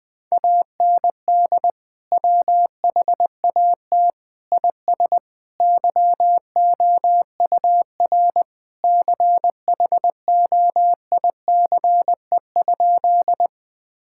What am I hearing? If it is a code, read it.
Morse code, heard as a keyed tone, about 20 wpm, 700 Hz. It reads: AND WHAT IS YOUR CHOICE?
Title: Sentences from Top 1000 Words - Review Part 6 - 20wpm